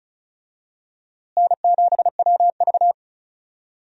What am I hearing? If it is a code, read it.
Morse code, heard as a keyed tone, about 36 wpm, 700 Hz. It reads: N7WV
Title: USA Call Signs - 4 Characters - Repeated 2x - 35wpm